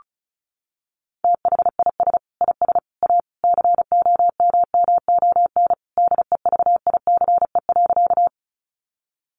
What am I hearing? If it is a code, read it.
Morse code, heard as a keyed tone, about 34 wpm, 700 Hz. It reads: THIS IS A COMMON DEVICE.
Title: Sentences from Top 1500 Words 35wpm